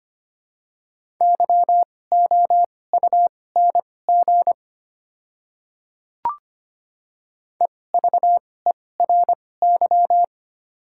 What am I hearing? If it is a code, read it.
Morse code, heard as a keyed tone, about 25 wpm, spaced as for 18 wpm, 700 Hz. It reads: YOUNG EVERY
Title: 2x Character Spacing - Top 500 Words - 25wpm